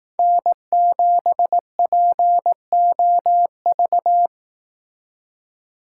Morse code, heard as a keyed tone, about 18 wpm, 700 Hz. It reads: N7POV
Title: Speed Racing - USA Call Signs 15wpm